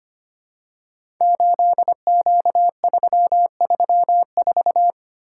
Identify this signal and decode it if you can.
Morse code, heard as a keyed tone, about 25 wpm, 700 Hz. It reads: 8Q334
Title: Code Groups - 5 Letter:Numbers - 25wpm